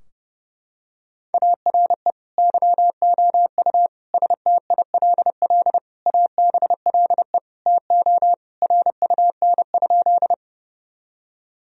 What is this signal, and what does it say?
Morse code, heard as a keyed tone, about 30 wpm, 700 Hz. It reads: ARE YOU STILL ABLE TO RUN?
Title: Sentences from Top 1000 Words - Review Part 2 - 30wpm